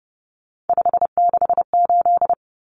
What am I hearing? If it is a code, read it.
Morse code, heard as a keyed tone, about 30 wpm, 700 Hz. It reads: 568
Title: Common Cut Numbers - 3-Digit 30wpm